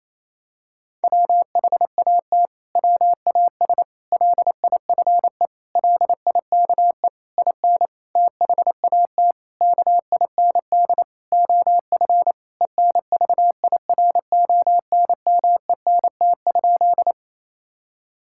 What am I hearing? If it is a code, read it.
Morse code, heard as a keyed tone, about 28 wpm, 700 Hz. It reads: WHAT WAS LIFE LIKE IN THAT KIND OF ENVIRONMENT?
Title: Sentences from Top 1000 Words - Review Part 5 - 28wpm